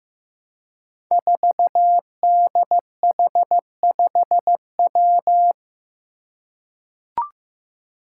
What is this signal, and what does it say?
Morse code, heard as a keyed tone, about 15 wpm, 700 Hz. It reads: 4DH5W
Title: Five Challenging Letter-Number Combinations 15wpm